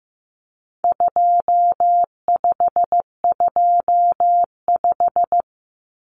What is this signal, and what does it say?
Morse code, heard as a keyed tone, about 15 wpm, 700 Hz. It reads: 2525
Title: Common Cut Numbers - 4-Digit 15wpm